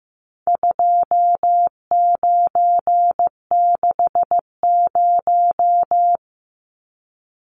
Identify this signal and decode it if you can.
Morse code, heard as a keyed tone, about 15 wpm, 700 Hz. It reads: 2960
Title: Numbers - 4-Digit 15wpm